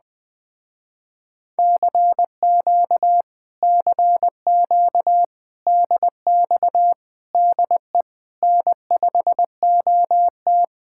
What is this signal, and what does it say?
Morse code, heard as a keyed tone, about 20 wpm, 700 Hz. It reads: CQ CQ DX DE N5OT
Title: Speed-Racing Common QSO Phrases Encore - 17wpm